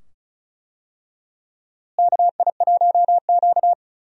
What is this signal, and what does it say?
Morse code, heard as a keyed tone, about 36 wpm, 700 Hz. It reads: KI1Q
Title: USA Call Signs - 4 Characters - Repeated 2x - 35wpm